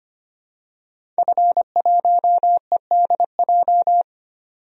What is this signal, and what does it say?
Morse code, heard as a keyed tone, about 25 wpm, 700 Hz. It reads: F1EDJ
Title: DXCC-100 Call Signs - 5 Characters - 25wpm